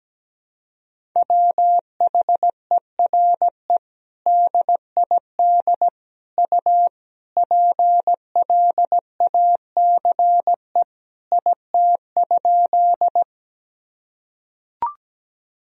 Morse code, heard as a keyed tone, about 17 wpm, 700 Hz. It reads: WHERE DID U PLACE IT?
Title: Sentences from Top 200 Words and Abbreviations 17wpm